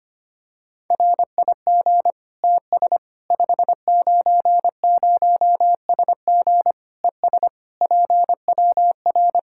Morse code, heard as a keyed tone, about 25 wpm, 700 Hz. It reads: RIG TS 590SG ES PWR
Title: Practice QSO - Exchange 2 - Info - 25wpm